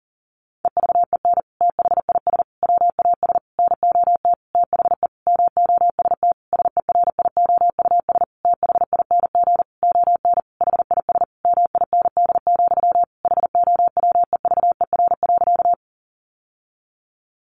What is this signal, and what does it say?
Morse code, heard as a keyed tone, about 40 wpm, 700 Hz. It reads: EVEN THIS WAS NOT THE MOST SERIOUS THING ON HIS MIND, HOWEVER.
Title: Sentences from Top 1100 Words 40wpm